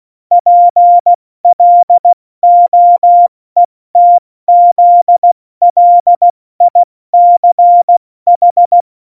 Morse code, heard as a keyed tone, about 16 wpm, spaced as for 14 wpm, 700 Hz. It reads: PLOETZLICH
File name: Morsekurs / Witze / MP3, Witz 24 April 2018, 70 F80